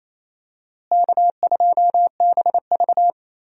Morse code, heard as a keyed tone, about 28 wpm, 700 Hz. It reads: K2BV
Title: USA Call Signs - 4 Characters - Repeated 2x - 28wpm